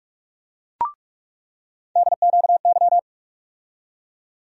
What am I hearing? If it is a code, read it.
Morse code, heard as a keyed tone, about 46 wpm, 700 Hz. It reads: DQY
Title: USA Call Sign Postfixes - 3 Letters 45wpm